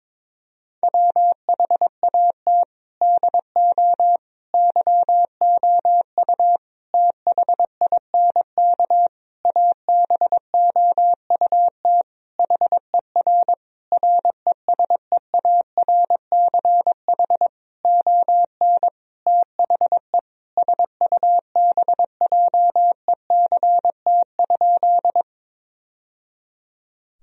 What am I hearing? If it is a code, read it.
Morse code, heard as a keyed tone, about 22 wpm, 700 Hz. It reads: WHAT DO YOU THINK ABOUT HER RESEARCH ON THE SUBJECT?
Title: Sentences from Top 1000 Words - Review Part 4 - 22wpm